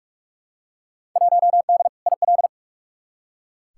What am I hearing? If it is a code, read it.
Morse code, heard as a keyed tone, about 46 wpm, 700 Hz. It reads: 1D IL
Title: Field Day Exchanges - 45wpm